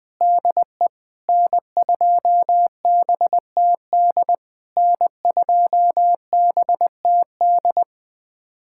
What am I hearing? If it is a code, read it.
Morse code, heard as a keyed tone, about 20 wpm, 700 Hz. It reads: DE N2BTD N2BTD
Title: POTA QSO Practice - 20wpm